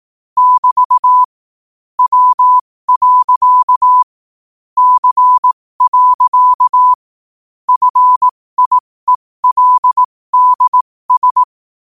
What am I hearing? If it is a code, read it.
Morse code, heard as a keyed tone, about 18 wpm, spaced as for 15 wpm, 1000 Hz. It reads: <BT> W. C. FIELDS
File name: qotdfiles / 15wpm